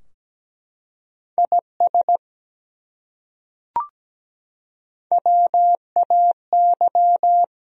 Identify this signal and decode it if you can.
Morse code, heard as a keyed tone, about 17 wpm, 700 Hz. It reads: IS WAY
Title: Top 100 Words Repeated 17wpm